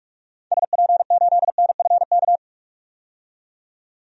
Morse code, heard as a keyed tone, about 41 wpm, 700 Hz. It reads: SP8NFK
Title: DXCC-100 Call Signs - 6 Characters - 45wpm